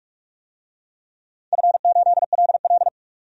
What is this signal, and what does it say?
Morse code, heard as a keyed tone, about 44 wpm, 700 Hz. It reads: F8LL
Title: DXCC-100 Call Signs - 4 Characters - 45wpm